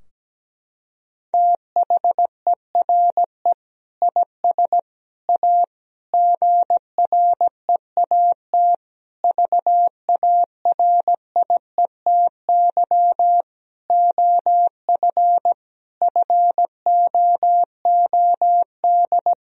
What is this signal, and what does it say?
Morse code, heard as a keyed tone, about 17 wpm, 700 Hz. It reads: THERE IS A GREAT VARIETY OF FOOD
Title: Sentences from Top 1000 Words - Review Part 4 - 17wpm